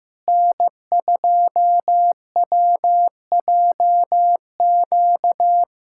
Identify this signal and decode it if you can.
Morse code, heard as a keyed tone, about 15 wpm, 700 Hz. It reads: N2WJQ
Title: USA Callsign DE Callsign 15wpm